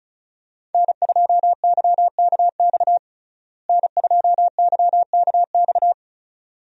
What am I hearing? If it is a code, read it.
Morse code, heard as a keyed tone, about 36 wpm, 700 Hz. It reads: N2YKX N2YKX
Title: USA Call Signs - 5 Characters - Repeated 2x - 35wpm